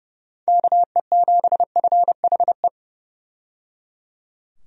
Morse code, heard as a keyed tone, about 30 wpm, 700 Hz. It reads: KE7FHE
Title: USA Call Signs - 6 Characters - 30wpm